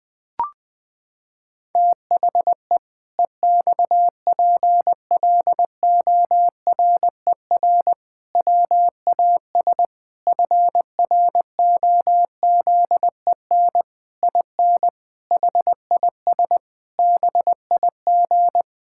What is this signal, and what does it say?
Morse code, heard as a keyed tone, about 19 wpm, 700 Hz. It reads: THE EXPLORER WAS FROZEN IN HIS BIG
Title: Pangram Challenge v2 - 20wpm